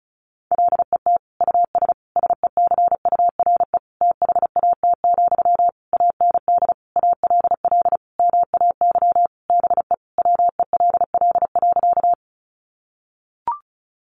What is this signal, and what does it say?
Morse code, heard as a keyed tone, about 34 wpm, 700 Hz. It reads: LET US SECURE THAT, AND ALL MAY BE WELL.
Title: Sentences from Top 1200 Words 35wpm